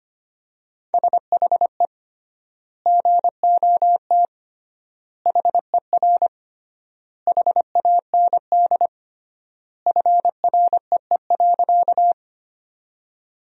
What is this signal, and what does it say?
Morse code, heard as a keyed tone, about 25 wpm, 700 Hz. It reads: SHE GOT HER HAND FREE.
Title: 3x Word-Spacing - Sentences from Top 500 Words - 25wpm